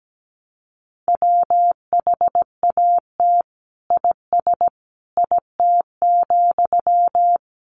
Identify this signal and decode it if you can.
Morse code, heard as a keyed tone, about 17 wpm, 700 Hz. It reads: WHAT IS IT,
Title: Sentences from Top 1200 Words 17wpm